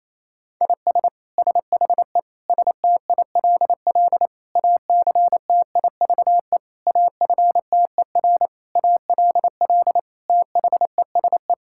Morse code, heard as a keyed tone, about 28 wpm, 700 Hz. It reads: IS SHE STILL ACTIVE AFTER ALL THESE
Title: Sentences from Top 1000 Words - Review Part 6 - 28wpm